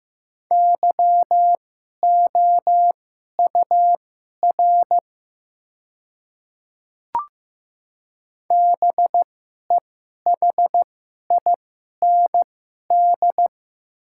Morse code, heard as a keyed tone, about 15 wpm, spaced as for 11 wpm, 700 Hz. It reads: YOUR BEHIND
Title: 2x Character Spacing - Top 500 Words - 15wpm